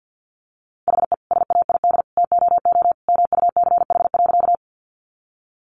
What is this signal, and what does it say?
Morse code, heard as a keyed tone, about 50 wpm, 700 Hz. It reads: HE SAID TOO MUCH.
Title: Sentences from Top 200 Words 50wpm